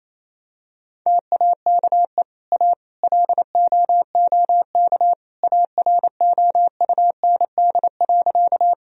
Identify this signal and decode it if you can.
Morse code, heard as a keyed tone, about 27 wpm, 700 Hz. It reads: TAKE A LOOK AROUND.
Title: Sentences from Top 1000 Words - Review Part 3 - 28wpm